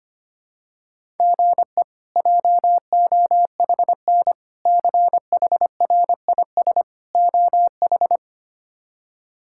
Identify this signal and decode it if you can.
Morse code, heard as a keyed tone, about 25 wpm, 700 Hz. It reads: GE JOHN CHRIS OH
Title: K1USN SST QSO Practice - 25wpm